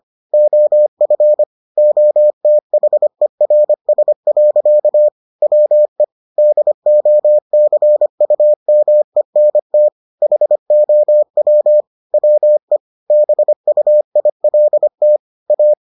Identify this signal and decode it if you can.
Morse code, heard as a keyed tone, about 25 wpm, 600 Hz. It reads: OF OTHERS. WE DOCUMENT HOW WE BUILT A